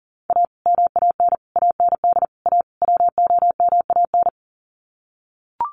Morse code, heard as a keyed tone, about 40 wpm, 700 Hz. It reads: A MAN AND A WOMAN
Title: Sets of 5 Words 40wpm